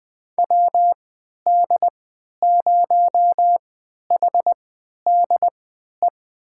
Morse code, heard as a keyed tone, about 20 wpm, spaced as for 11 wpm, 700 Hz. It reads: WD0HDE
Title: USA Call Signs - 3x Character Spacing - 20wpm